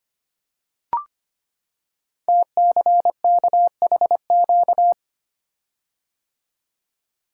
Morse code, heard as a keyed tone, about 25 wpm, 690 Hz. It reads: TCKHQ
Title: Code Groups - 5 Letters - 25wpm